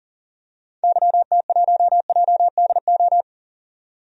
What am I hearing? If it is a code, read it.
Morse code, heard as a keyed tone, about 37 wpm, 700 Hz. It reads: YT1JDO
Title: DXCC-100 Call Signs - 6 Characters - 40wpm